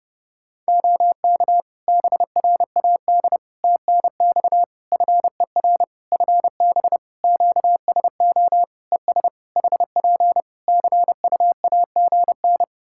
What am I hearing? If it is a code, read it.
Morse code, heard as a keyed tone, about 30 wpm, 700 Hz. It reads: OK BRAD TNX FER FB QSO ES HP CUAGN
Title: K1USN SST QSO Practice 30wpm